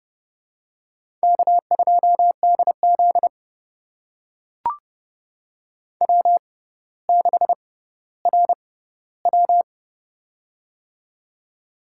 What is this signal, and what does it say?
Morse code, heard as a keyed tone, about 30 wpm, spaced as for 15 wpm, 690 Hz. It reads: K2DZ W6RW
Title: USA Call Signs - 4 Characters - 6x Character Spacing - 30wpm